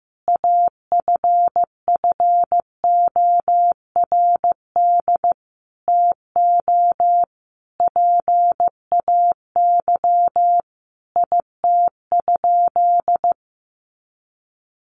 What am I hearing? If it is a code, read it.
Morse code, heard as a keyed tone, about 15 wpm, 700 Hz. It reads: AFFORD TO PAY IT?
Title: Sentences from Top 1100 Words 15wpm